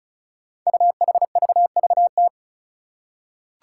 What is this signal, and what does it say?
Morse code, heard as a keyed tone, about 36 wpm, 700 Hz. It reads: UHVVT
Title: Five Challenging Letter-Number Combinations 35wpm